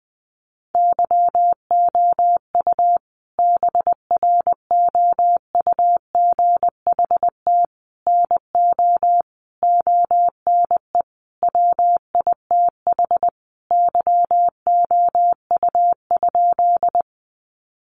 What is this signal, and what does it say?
Morse code, heard as a keyed tone, about 20 wpm, 700 Hz. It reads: YOU BROUGHT NO ONE WITH YOU?
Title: Sentences from Top 500 Words 20wpm